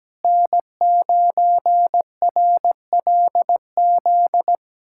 Morse code, heard as a keyed tone, about 17 wpm, 700 Hz. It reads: N9RLZ
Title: USA Callsign DE Callsign - 5 Characters - 17wpm